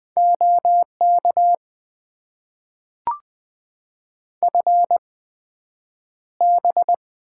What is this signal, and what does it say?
Morse code, heard as a keyed tone, about 20 wpm, spaced as for 9 wpm, 700 Hz. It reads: OK FB
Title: Top 90 QSO Elements - 8x Character Spacing - 20wpm